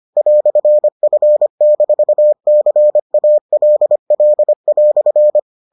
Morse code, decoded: "F-CALL"

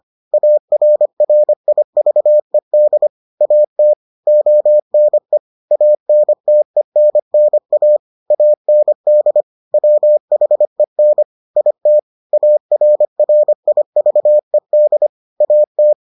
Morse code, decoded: ARRIVED AT ONE ANTENNA AND WHEN IT ARRIVED AT

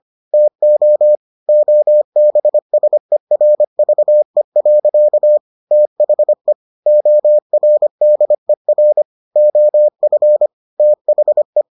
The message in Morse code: TO OBSERVE. THE ORDER OF THE

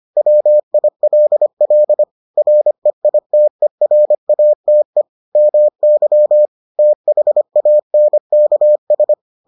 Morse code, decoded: WILL REITERATE MY THANKS

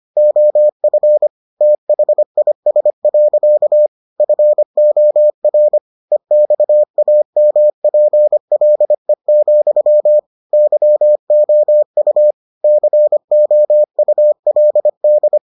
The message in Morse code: OF THIS. FOR EXAMPLE, YOU COULD

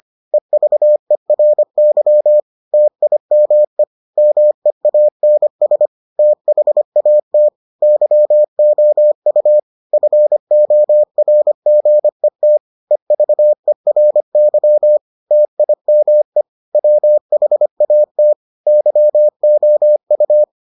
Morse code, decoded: EVERY TIME MEANS THAT YOU FORGET EVERY TIME WHAT YOU

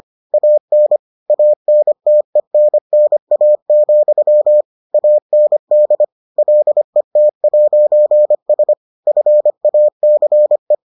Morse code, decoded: AN ANTENNA, AND LET'S FACE

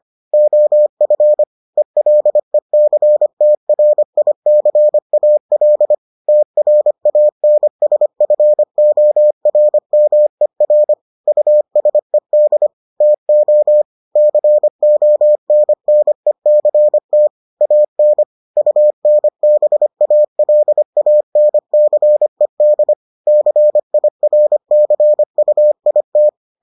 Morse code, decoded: OF ELECTRICAL TRANSFORMER USED TO CONNECT AN UNBALANCED CIRCUIT

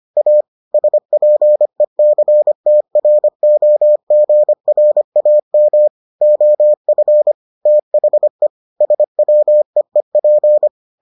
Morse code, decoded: A SPECTROGRAM OF THE SWEEP

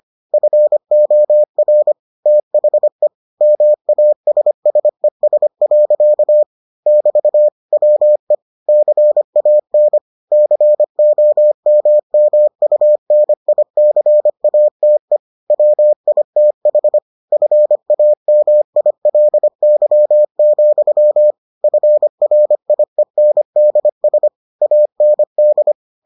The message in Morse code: FOR THE MASSES. <BT> WE CAN COMMUNICATE WITH FAMILY, FRIENDS AND

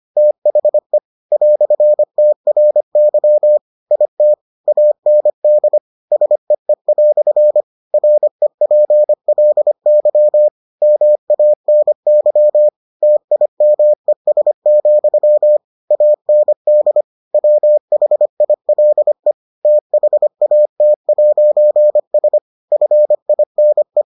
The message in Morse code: THE "TRY IT AND SEE" REPLY MANY TIMES, AND WHILE THAT'S FINE

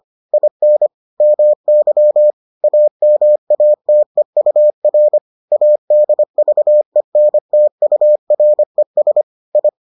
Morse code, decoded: IN MY AMATEUR ADVENTURES I

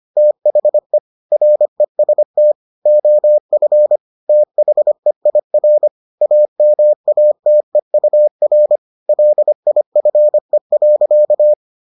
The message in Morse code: THE REST OF THEIR AMATEUR LIFE.

